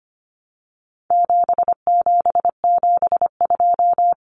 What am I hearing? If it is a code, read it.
Morse code, heard as a keyed tone, about 25 wpm, 700 Hz. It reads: 7772
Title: Numbers - 4-Digit 25wpm